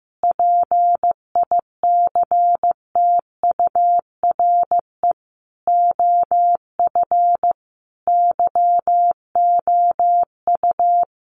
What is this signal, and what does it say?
Morse code, heard as a keyed tone, about 15 wpm, 700 Hz. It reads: PICTURE OF YOU
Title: A Sound Of Thunder 15wpm